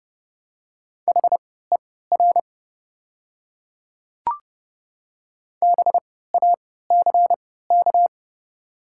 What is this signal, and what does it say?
Morse code, heard as a keyed tone, about 31 wpm, spaced as for 17 wpm, 700 Hz. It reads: HER BACK
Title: 3x Character Spacing - Top 200 Words - 30wpm